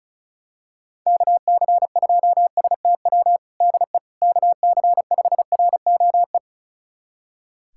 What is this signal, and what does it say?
Morse code, heard as a keyed tone, about 34 wpm, 700 Hz. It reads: KC2STW DE KC5ROE
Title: USA Callsign DE Callsign - 6 Characters - 35wpm